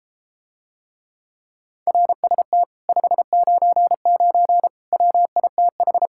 Morse code, heard as a keyed tone, about 33 wpm, 700 Hz. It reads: RST 599 WITH